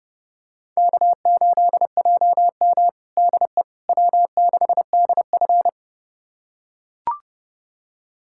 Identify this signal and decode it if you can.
Morse code, heard as a keyed tone, about 30 wpm, 700 Hz. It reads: K8JM DE W6DF
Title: USA Callsign DE Callsign - 4 Characters - 30wpm